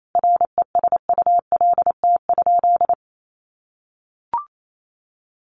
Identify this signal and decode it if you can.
Morse code, heard as a keyed tone, about 28 wpm, 700 Hz. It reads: RESULT?
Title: Sentences from Top 1500 Words 28wpm